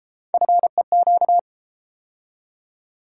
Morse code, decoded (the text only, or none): FEQ